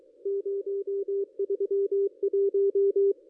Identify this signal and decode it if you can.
Morse code, heard as a keyed tone, about 22 wpm, 400 Hz. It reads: 031
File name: contests / ss-cw-k5tr-op-n5rz -1400